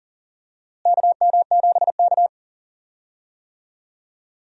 Morse code, heard as a keyed tone, about 40 wpm, 700 Hz. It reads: KM7K
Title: USA Call Signs - 4 Characters - Repeated 3x - 40wpm